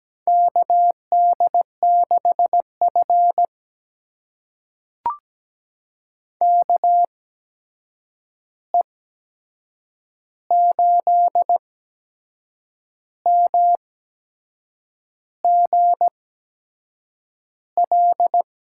Morse code, heard as a keyed tone, about 17 wpm, spaced as for 6 wpm, 700 Hz. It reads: KD6F KE8MGL
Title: USA Call Signs - 8x Character Spacing - 17wpm